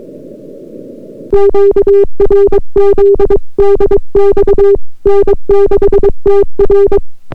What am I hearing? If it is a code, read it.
Morse code, heard as a keyed tone, about 22 wpm, 400 Hz. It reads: QRZ DX N6TR